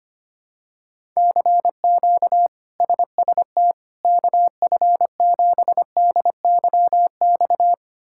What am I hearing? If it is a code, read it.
Morse code, heard as a keyed tone, about 25 wpm, 700 Hz. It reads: CQ SST KF7DYX
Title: K1USN SST QSO Practice - 25wpm